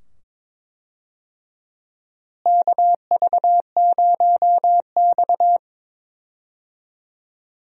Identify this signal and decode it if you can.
Morse code, heard as a keyed tone, about 22 wpm, 700 Hz. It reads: KV0X
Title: USA Call Signs - 4 Characters - Repeated 2x - 22wpm